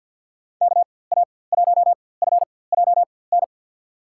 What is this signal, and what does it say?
Morse code, heard as a keyed tone, about 52 wpm, spaced as for 24 wpm, 700 Hz. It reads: KA1FJN
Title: USA Call Signs - 6 Characters - 4x Character Spacing - 50wpm